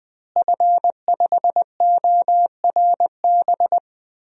Morse code, heard as a keyed tone, about 20 wpm, 700 Hz. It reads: F5ORB
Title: DXCC-100 Call Signs - 5 Characters - Repeated 2x - 20wpm